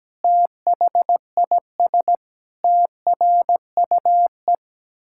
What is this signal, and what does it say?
Morse code, heard as a keyed tone, about 17 wpm, 700 Hz. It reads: THIS TRUE
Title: Pangram Challenge v2 - 17wpm